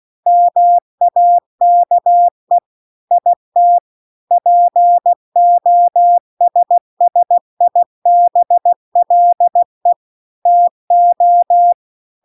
Morse code, heard as a keyed tone, about 16 wpm, 700 Hz. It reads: MAKE IT POSSIBLE TO